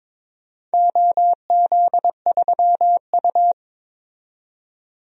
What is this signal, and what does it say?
Morse code, heard as a keyed tone, about 22 wpm, 700 Hz. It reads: OZ3U
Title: DXCC-100 Call Signs - 4 Characters - 22wpm